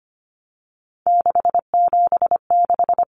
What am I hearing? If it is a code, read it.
Morse code, heard as a keyed tone, about 25 wpm, 700 Hz. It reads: 676
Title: Numbers - 3-Digit 25wpm